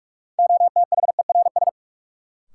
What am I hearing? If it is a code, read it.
Morse code, heard as a keyed tone, about 46 wpm, 700 Hz. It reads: OTHERS
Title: Sentences from Top 1500 Words - New Word Drill Alt - 45wpm